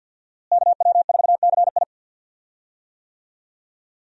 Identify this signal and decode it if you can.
Morse code, heard as a keyed tone, about 52 wpm, 700 Hz. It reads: KW4CI
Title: USA Call Signs - 5 Characters - Repeated 2x - 50wpm